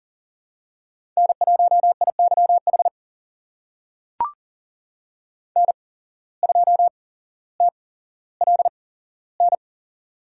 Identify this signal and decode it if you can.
Morse code, heard as a keyed tone, about 40 wpm, spaced as for 17 wpm, 700 Hz. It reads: N1IYH N2TLN
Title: USA Call Signs - 5 Characters - 8x Character Spacing - 40wpm